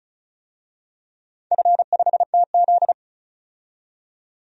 Morse code, heard as a keyed tone, about 36 wpm, 700 Hz. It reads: F5TZ
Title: DXCC-100 Call Signs - 4 Characters - 35wpm